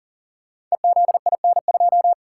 Morse code, heard as a keyed tone, about 38 wpm, 700 Hz. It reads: EZIN2
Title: Code Groups - 5 Letter:Numbers - 40wpm